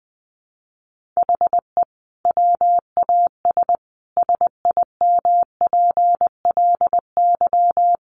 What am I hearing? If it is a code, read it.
Morse code, heard as a keyed tone, about 20 wpm, 700 Hz. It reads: HE WAS SIMPLY
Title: Sentences from Top 1500 Words 20wpm